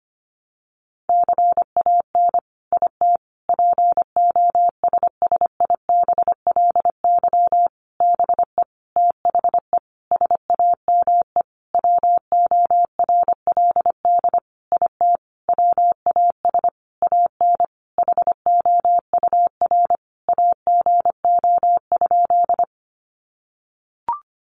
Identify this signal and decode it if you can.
Morse code, heard as a keyed tone, about 25 wpm, 700 Hz. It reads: CAN IT POSSIBLY BE THE SAME WORLD IT WAS AN HOUR AGO?
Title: Sentences from Top 1300 Words 25wpm